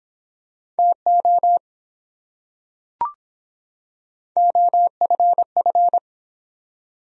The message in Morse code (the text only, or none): TO OFF